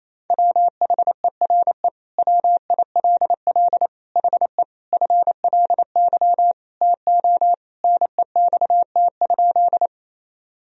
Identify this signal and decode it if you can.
Morse code, heard as a keyed tone, about 28 wpm, 700 Hz. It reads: WHERE WILL HE FLY TO NEXT?
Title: Sentences from Top 1000 Words - Review Part 2 - 28wpm